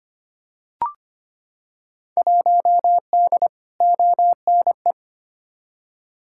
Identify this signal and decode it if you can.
Morse code, heard as a keyed tone, about 25 wpm, 700 Hz. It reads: 1D ONE
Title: Field Day Exchanges - 25wpm